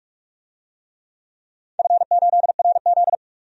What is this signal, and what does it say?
Morse code, heard as a keyed tone, about 42 wpm, 700 Hz. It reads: F8RZ
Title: DXCC-100 Call Signs - 4 Characters - 45wpm